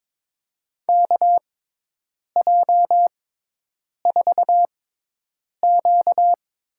Morse code, heard as a keyed tone, about 22 wpm, spaced as for 8 wpm, 700 Hz. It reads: KJ4Q
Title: USA Call Signs - 5 Characters - 6x Character Spacing - 22wpm